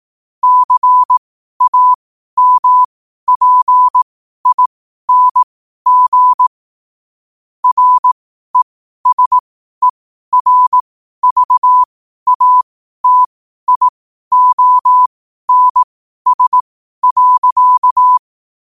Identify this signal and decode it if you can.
Morse code, heard as a keyed tone, about 18 wpm, spaced as for 12 wpm, 1000 Hz. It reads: CAMPING RESERVATIONS.
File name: qotdfiles / 12wpm